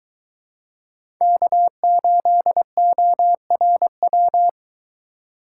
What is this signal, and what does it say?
Morse code, heard as a keyed tone, about 20 wpm, 700 Hz. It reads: K8ORW